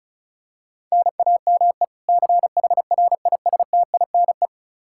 Morse code, heard as a keyed tone, about 36 wpm, 700 Hz. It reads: NAME CHRISTINE
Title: Practice QSO - Exchange 1 - Beginning - 35wpm